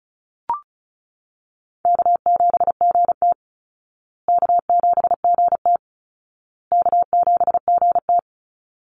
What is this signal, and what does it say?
Morse code, heard as a keyed tone, about 34 wpm, 700 Hz. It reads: K7GT K7GT K7GT
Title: Call Signs Repeated 2x - 35wpm